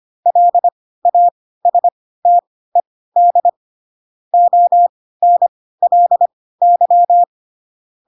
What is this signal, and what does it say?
Morse code, heard as a keyed tone, about 25 wpm, spaced as for 16 wpm, 700 Hz. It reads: LASTED ONLY
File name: Pinocchio / Pinocchio ch4 10 25 ik1hge